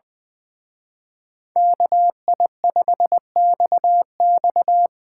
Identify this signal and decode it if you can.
Morse code, heard as a keyed tone, about 21 wpm, 700 Hz. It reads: KI5XX
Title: Speed Racing - USA 2x2 Call Signs 20wpm